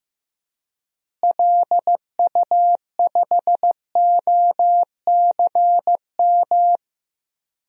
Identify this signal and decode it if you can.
Morse code, heard as a keyed tone, about 15 wpm, 700 Hz. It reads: LU5OCM
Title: DXCC-100 Call Signs - 6 Characters - Repeated 2x - 15wpm